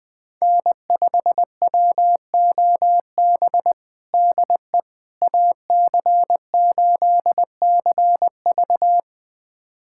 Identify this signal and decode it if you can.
Morse code, heard as a keyed tone, about 20 wpm, 700 Hz. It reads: N5WOB DE AC8CV
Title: USA Callsign DE Callsign - 5 Characters - 20wpm